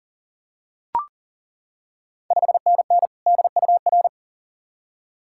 Morse code, heard as a keyed tone, about 40 wpm, 690 Hz. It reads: 5NN DUR
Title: Ontario QSO Party Exchange 40wpm